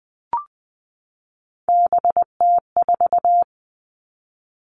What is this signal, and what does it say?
Morse code, heard as a keyed tone, about 20 wpm, 700 Hz. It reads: BT4
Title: Three Challenging Letter-Number Combinations 20wpm